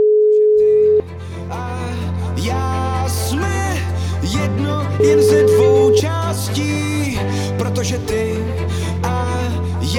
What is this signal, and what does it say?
Ringback tone (Europe, 425 Hz), with speech.